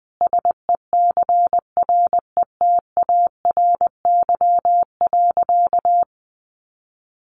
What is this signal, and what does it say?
Morse code, heard as a keyed tone, about 20 wpm, 700 Hz. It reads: SECRETARY.
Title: Sentences from Top 1300 Words 20wpm